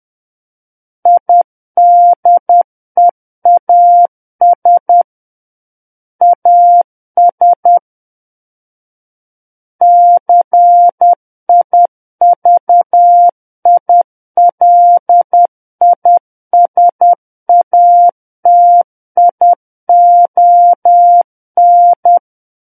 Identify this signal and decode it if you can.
Morse code, heard as a keyed tone, about 10 wpm, 700 Hz. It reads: IDEAS AS CIVILISATION
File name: Book One / 01-Chapter 1 - 10 WPM